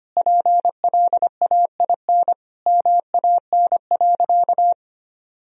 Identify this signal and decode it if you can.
Morse code, heard as a keyed tone, about 25 wpm, 700 Hz. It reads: PLAIN MAN.
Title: Sentences from Top 1000 Words - Review Part 2 - 25wpm